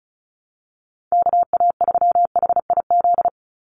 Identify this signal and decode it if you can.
Morse code, heard as a keyed tone, about 34 wpm, 700 Hz. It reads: KA3HIZ
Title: USA Call Signs 35wpm